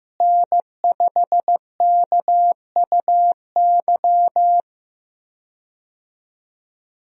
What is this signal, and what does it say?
Morse code, heard as a keyed tone, about 16 wpm, 700 Hz. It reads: N5KUY
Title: USA Call Signs - 5 Characters - Repeated 3x - 15wpm